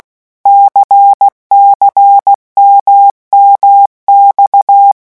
Morse code, heard as a keyed tone, about 16 wpm, 800 Hz. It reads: CCMMX
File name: morse mp3 / revision1-vitesse 15 mots